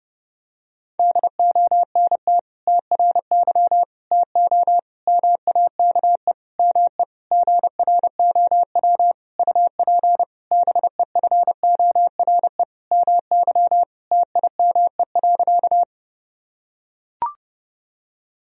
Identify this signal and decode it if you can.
Morse code, heard as a keyed tone, about 32 wpm, 700 Hz. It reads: DONT TRY TO MAKE ME GROW UP BEFORE MY TIME.
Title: Sentences from Top 1000 Words - Review Part 1 - 30wpm